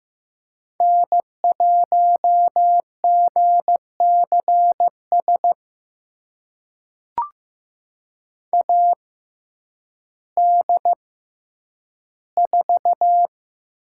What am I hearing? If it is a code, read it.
Morse code, heard as a keyed tone, about 15 wpm, spaced as for 9 wpm, 700 Hz. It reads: N1GCS AD4
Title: USA Call Signs - 5 Characters - 6x Character Spacing - 15wpm